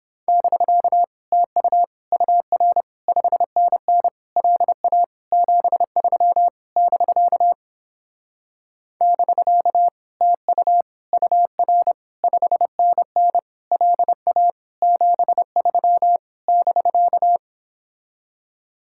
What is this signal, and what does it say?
Morse code, heard as a keyed tone, about 31 wpm, 700 Hz. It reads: <BK> TU UR 5NN LA 73 <BK> <BK> TU UR 5NN LA 73 <BK>